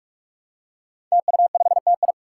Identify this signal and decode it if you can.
Morse code, heard as a keyed tone, about 46 wpm, 700 Hz. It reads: TU5TI
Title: Five Challenging Letter-Number Combinations 45wpm